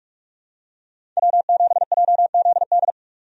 Morse code, heard as a keyed tone, about 43 wpm, 700 Hz. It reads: W7JZD
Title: USA Call Signs - 5 Characters - 45wpm